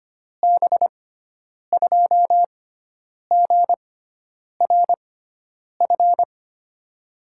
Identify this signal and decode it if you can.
Morse code, heard as a keyed tone, about 25 wpm, spaced as for 9 wpm, 700 Hz. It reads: B2GRF
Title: USA Call Signs - 6 Characters - 6x Character Spacing - 25wpm